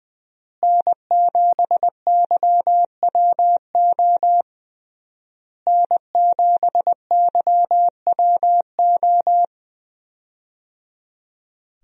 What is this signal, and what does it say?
Morse code, heard as a keyed tone, about 21 wpm, 700 Hz. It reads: N7YWO N7YWO
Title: USA Call Signs - 5 Characters - Repeated 3x - 20wpm